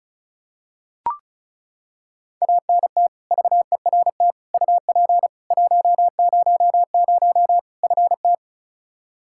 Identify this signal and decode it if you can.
Morse code, heard as a keyed tone, about 35 wpm, 700 Hz. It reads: ANT VERT UP 100 FT
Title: Practice QSO - Exchange 2 - Info - 35wpm